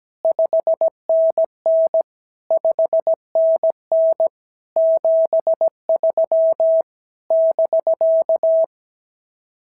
Morse code, heard as a keyed tone, about 17 wpm, 650 Hz. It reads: 5NN 5NN 73 <BK>
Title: SOTA QSO Practice 17wpm